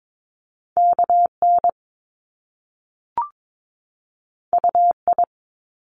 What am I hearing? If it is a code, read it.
Morse code, heard as a keyed tone, about 22 wpm, 700 Hz. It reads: KN UI
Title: Top Two Letter Combinations in Words 22wpm